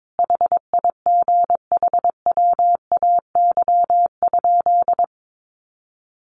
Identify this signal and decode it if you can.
Morse code, heard as a keyed tone, about 22 wpm, 700 Hz. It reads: HIGHWAY?
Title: Sentences from Top 1400 Words 22wpm